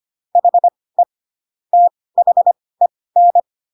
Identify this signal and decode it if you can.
Morse code, heard as a keyed tone, about 25 wpm, spaced as for 18 wpm, 700 Hz. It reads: HE THEN
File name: Pinocchio / Pinocchio ch1 12 25 ik1hge